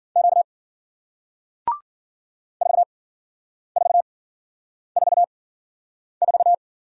Morse code, heard as a keyed tone, about 54 wpm, spaced as for 9 wpm, 700 Hz. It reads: Q 4444